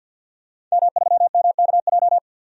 Morse code, heard as a keyed tone, about 47 wpm, 700 Hz. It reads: M3MKJ
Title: DXCC-100 Call Signs - 5 Characters - Repeated 3x - 50wpm